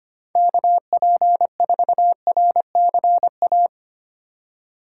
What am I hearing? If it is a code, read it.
Morse code, heard as a keyed tone, about 31 wpm, 700 Hz. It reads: KP4RCA
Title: Speed Racing - USA 2x3 Call Signs 25wpm